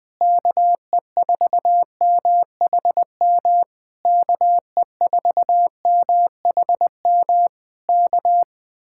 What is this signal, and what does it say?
Morse code, heard as a keyed tone, about 20 wpm, 700 Hz. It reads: KE4MHM KE4MHM K
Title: Speed-Racing Common QSO Phrases - 17wpm